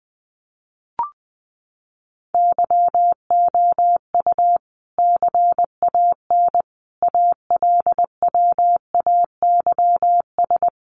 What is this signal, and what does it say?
Morse code, heard as a keyed tone, about 20 wpm, 700 Hz. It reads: YOU CAN ALWAYS
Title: Sentences from Top 1200 Words 20wpm